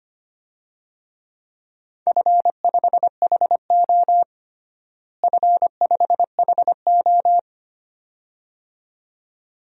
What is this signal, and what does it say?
Morse code, heard as a keyed tone, about 25 wpm, 700 Hz. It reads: F5HO F5HO
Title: DXCC-100 Call Signs - 4 Characters - Repeated 2x - 25wpm